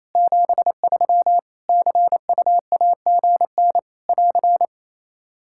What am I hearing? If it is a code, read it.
Morse code, heard as a keyed tone, about 28 wpm, 700 Hz. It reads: 73 CUAGN <AR>
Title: K1USN SST QSO Practice 28wpm